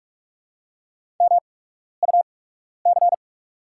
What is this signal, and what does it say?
Morse code, heard as a keyed tone, about 48 wpm, spaced as for 12 wpm, 700 Hz. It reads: MUC
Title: 8x Character Spacing - Top 300 Words - 45wpm